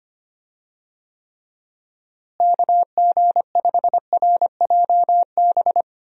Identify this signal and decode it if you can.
Morse code, heard as a keyed tone, about 25 wpm, 700 Hz. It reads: KG5RJB
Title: USA Call Signs - 6 Characters - Repeated 3x - 25wpm